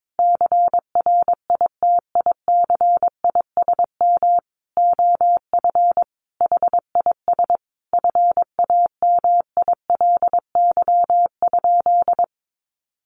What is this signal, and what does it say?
Morse code, heard as a keyed tone, about 22 wpm, 700 Hz. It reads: CRITICISM OF HIS FAMILY?
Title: Sentences from Top 1300 Words 22wpm